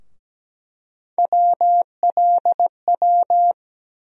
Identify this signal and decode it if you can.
Morse code, heard as a keyed tone, about 17 wpm, 700 Hz. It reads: WLW
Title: USA Call Sign Postfixes - 3 Letters 17wpm